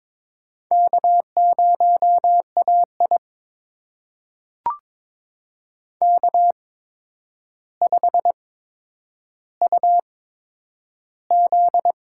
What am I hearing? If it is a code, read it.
Morse code, heard as a keyed tone, about 22 wpm, spaced as for 9 wpm, 700 Hz. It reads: K0AI K5UZ
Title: USA Call Signs - 4 Characters - 8x Character Spacing - 22wpm